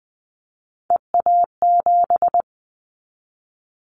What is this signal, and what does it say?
Morse code, heard as a keyed tone, about 20 wpm, 700 Hz. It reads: EA7
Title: Callsign Prefixes for Top 25 DXCC - 20wpm